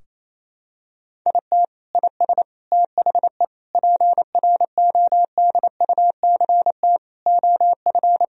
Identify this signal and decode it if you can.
Morse code, heard as a keyed tone, about 28 wpm, 700 Hz. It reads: IT IS THE PRODUCT OF
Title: Sentences from Top 1000 Words - Review Part 2 - 28wpm